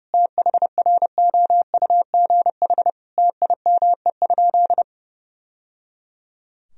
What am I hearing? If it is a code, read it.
Morse code, heard as a keyed tone, about 30 wpm, 700 Hz. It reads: THROUGH TIME?
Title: Sentences from Top 1000 Words - Review Part 2 - 30wpm